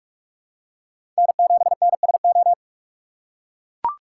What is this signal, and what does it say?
Morse code, heard as a keyed tone, about 42 wpm, 700 Hz. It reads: N7NSO
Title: USA Call Signs - 5 Characters - 2x Character Spacing - 45wpm